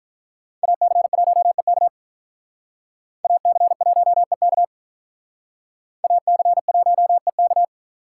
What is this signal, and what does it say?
Morse code, heard as a keyed tone, about 43 wpm, 700 Hz. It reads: AC1EK AC1EK AC1EK
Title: Speed Racing - USA Call Signs 35wpm